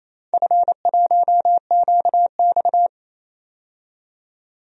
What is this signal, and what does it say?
Morse code, heard as a keyed tone, about 28 wpm, 700 Hz. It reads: F1QX
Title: DXCC-100 Call Signs - 4 Characters - 28wpm